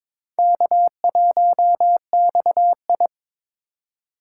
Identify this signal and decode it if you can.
Morse code, heard as a keyed tone, about 26 wpm, 700 Hz. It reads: K1XI
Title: Speed Racing - USA 1x2 Call Signs 22wpm